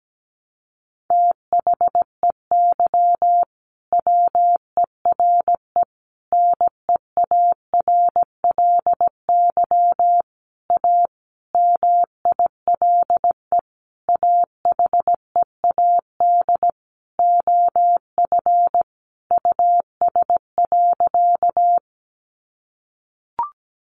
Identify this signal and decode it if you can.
Morse code, heard as a keyed tone, about 17 wpm, 700 Hz. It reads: THEY WERE NEARLY A MILE AHEAD OF US.
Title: Sentences from Top 1300 Words 17wpm